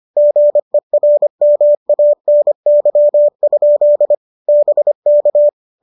Morse code, heard as a keyed tone, about 25 wpm, 590 Hz. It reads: GERMANY? BK
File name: qso218 25